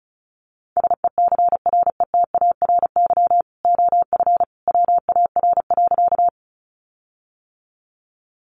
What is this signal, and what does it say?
Morse code, heard as a keyed tone, about 34 wpm, 700 Hz. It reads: SECRETARY OF WAR.
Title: Sentences from Top 1300 Words 35wpm